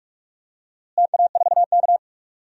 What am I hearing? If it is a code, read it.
Morse code, heard as a keyed tone, about 43 wpm, 700 Hz. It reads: TA4K
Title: DXCC-100 Call Signs - 4 Characters - Repeated 2x - 45wpm